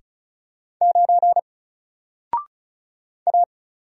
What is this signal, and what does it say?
Morse code, heard as a keyed tone, about 36 wpm, 700 Hz. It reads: 9 A
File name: Single-Letter-Number-35wpm